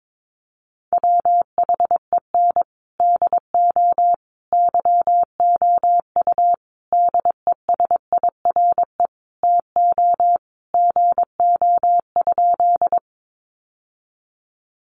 Morse code, heard as a keyed tone, about 22 wpm, 700 Hz. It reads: WHEN DO YOU DESIRE TO GO?
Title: Sentences from Top 1100 Words 22wpm